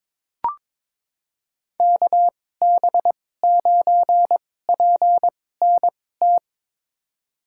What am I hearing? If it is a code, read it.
Morse code, heard as a keyed tone, about 22 wpm, spaced as for 16 wpm, 700 Hz. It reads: KB9PNT
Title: USA Call Signs - 6 Characters - 2x Character Spacing - 22wpm